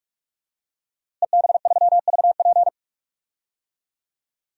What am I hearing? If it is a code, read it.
Morse code, heard as a keyed tone, about 46 wpm, 700 Hz. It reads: EB3VP
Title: DXCC-100 Call Signs - 5 Characters - Repeated 3x - 45wpm